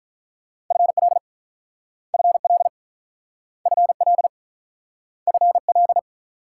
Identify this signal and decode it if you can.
Morse code, heard as a keyed tone, about 48 wpm, 700 Hz. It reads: FL FL FL FL